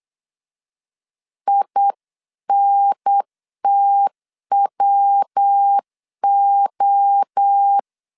Morse code, heard as a keyed tone, about 8 wpm, 800 Hz. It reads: INTWO